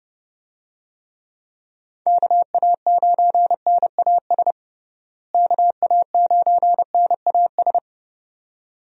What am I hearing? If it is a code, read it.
Morse code, heard as a keyed tone, about 32 wpm, 700 Hz. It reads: KA9NAS KA9NAS
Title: USA Call Signs - 6 Characters - Repeated 2x - 30wpm